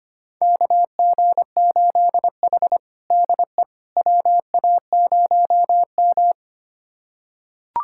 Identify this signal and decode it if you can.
Morse code, heard as a keyed tone, about 25 wpm, 700 Hz. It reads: KG8H DE WA0M
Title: USA Callsign DE Callsign - 4 Characters - 25wpm